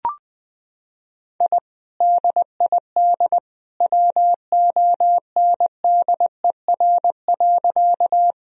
Morse code, heard as a keyed tone, about 20 wpm, 700 Hz. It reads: I DID WONDER.
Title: Sentences from Top 1000 Words - Review Part 3 - 20wpm